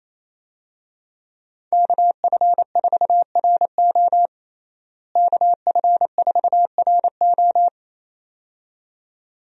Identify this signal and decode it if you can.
Morse code, heard as a keyed tone, about 27 wpm, 700 Hz. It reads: KF4RO KF4RO
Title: USA Call Signs - 5 Characters - Repeated 2x - 28wpm